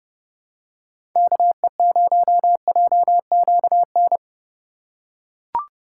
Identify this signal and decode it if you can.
Morse code, heard as a keyed tone, about 30 wpm, 700 Hz. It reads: KE0JQN
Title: USA Call Signs - 8x Character Spacing - 30wpm